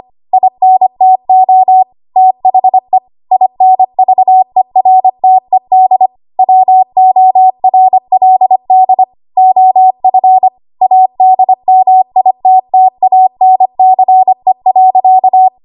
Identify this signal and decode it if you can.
Morse code, heard as a keyed tone, about 25 wpm, 750 Hz. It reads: INTO THE INVERTED WORLD OF ADMITTANCE.
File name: Archive / 25 WPM / 25WPM